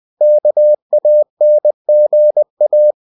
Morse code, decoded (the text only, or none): KANGA